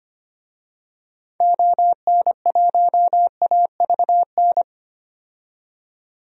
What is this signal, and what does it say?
Morse code, heard as a keyed tone, about 25 wpm, 700 Hz. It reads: ON1AVN
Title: DXCC-100 Call Signs - 6 Characters - 25wpm